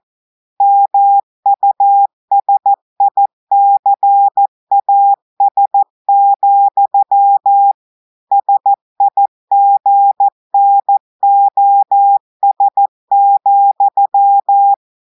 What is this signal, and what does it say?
Morse code, heard as a keyed tone, about 14 wpm, 800 Hz. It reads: MUSICAS, SIGNOS,
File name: audios / portugais / portugais texte en clair 2 vitesse 14 mots minutes